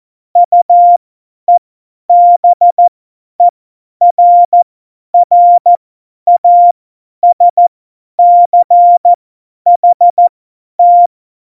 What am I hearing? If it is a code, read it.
Morse code, heard as a keyed tone, about 14 wpm, spaced as for 10 wpm, 700 Hz. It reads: UEBERRASCHT
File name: Morsekurs / Witze / Witz 11. Februar 2020, 50 F70